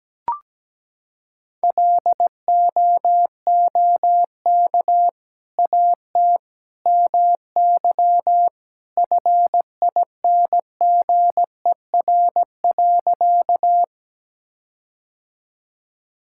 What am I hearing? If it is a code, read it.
Morse code, heard as a keyed tone, about 17 wpm, 700 Hz. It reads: LOOK AT MY FINGER.
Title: Sentences from Top 1600 Words 17wpm